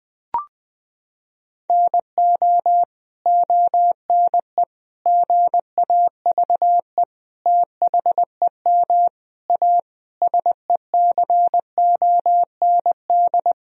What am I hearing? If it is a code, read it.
Morse code, heard as a keyed tone, about 19 wpm, 700 Hz. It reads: NO ONE GAVE THEM A SECOND
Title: Sentences from Top 1000 Words - Review Part 1 - 20wpm